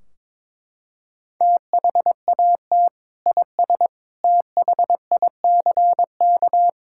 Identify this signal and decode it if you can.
Morse code, heard as a keyed tone, about 22 wpm, 700 Hz. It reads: THAT IS THICK
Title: Sentences from Top 1600 Words 22wpm